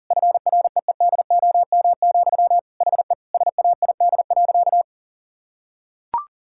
Morse code, heard as a keyed tone, about 37 wpm, 700 Hz. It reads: FREEDOM, HE SAID.
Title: Sentences from Top 1000 Words - Review Part 5 - 40wpm